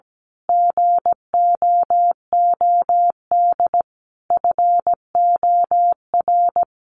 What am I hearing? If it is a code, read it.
Morse code, heard as a keyed tone, about 17 wpm, 700 Hz. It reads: GOOD FOR
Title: Sentences from Top 1400 Words 17wpm